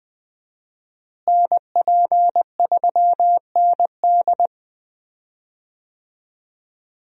Morse code, heard as a keyed tone, about 20 wpm, 700 Hz. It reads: NP3ND
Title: USA Call Signs - 5 Characters - Repeated 3x - 20wpm